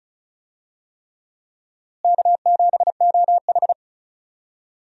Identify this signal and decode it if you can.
Morse code, heard as a keyed tone, about 36 wpm, 700 Hz. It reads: K7OH
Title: USA Call Signs - 4 Characters - Repeated 3x - 35wpm